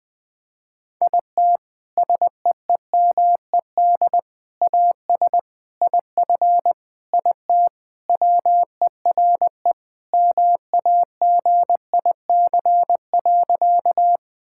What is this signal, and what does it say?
Morse code, heard as a keyed tone, about 20 wpm, 700 Hz. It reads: IT SEEMED AS IF IT WERE MAGIC.
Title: Sentences with 4-Word Phrases - 20wpm